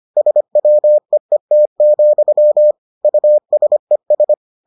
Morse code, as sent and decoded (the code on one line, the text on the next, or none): ... .-- . - --..-- / ..- ... ...
SWEET, USES